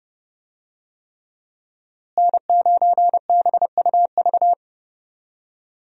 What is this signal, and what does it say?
Morse code, heard as a keyed tone, about 30 wpm, 700 Hz. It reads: N9BUV
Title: USA Call Signs - 5 Characters - Repeated 3x - 30wpm